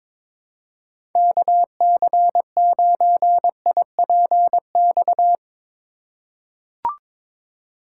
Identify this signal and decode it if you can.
Morse code, heard as a keyed tone, about 22 wpm, 700 Hz. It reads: KC9IPX
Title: USA Call Signs - 6 Characters - 2x Character Spacing - 22wpm